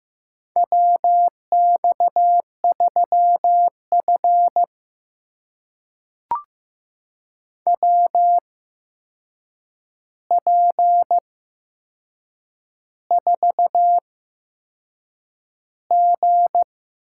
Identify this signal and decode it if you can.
Morse code, heard as a keyed tone, about 15 wpm, spaced as for 6 wpm, 700 Hz. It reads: WX3F WP4G
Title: USA Call Signs - 4 Characters - 8x Character Spacing - 15wpm